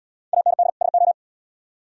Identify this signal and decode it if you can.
Morse code, heard as a keyed tone, about 70 wpm, 700 Hz. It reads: AND SO